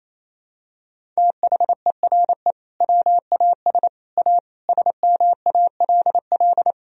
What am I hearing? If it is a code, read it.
Morse code, heard as a keyed tone, about 28 wpm, 700 Hz. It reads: THERE WAS A SMALL
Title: Sentences from Top 1000 Words - Review Part 1 - 28wpm